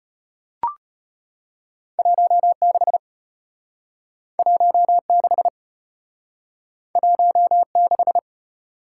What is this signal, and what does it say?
Morse code, heard as a keyed tone, about 35 wpm, 700 Hz. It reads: 16 16 16